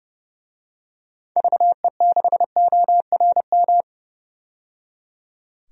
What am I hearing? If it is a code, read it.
Morse code, heard as a keyed tone, about 30 wpm, 700 Hz. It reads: VE6ORM
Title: DXCC-100 Call Signs - 6 Characters - 30wpm